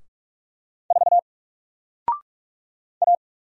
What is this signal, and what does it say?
Morse code, heard as a keyed tone, about 44 wpm, 700 Hz. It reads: 4 A
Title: Single Character (Letter:Number:Punctuation:Prosign) Alt 45wpm